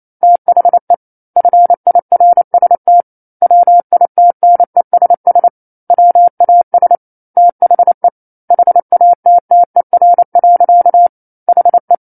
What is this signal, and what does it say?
Morse code, heard as a keyed tone, about 28 wpm, 700 Hz. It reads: THE FIRST WITNESS WAS THE HATTER. HE
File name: 09-Chapter-09 28 WPM